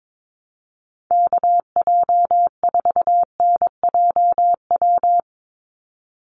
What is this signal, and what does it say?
Morse code, heard as a keyed tone, about 22 wpm, 700 Hz. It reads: KJ4NJW
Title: USA Call Signs 22wpm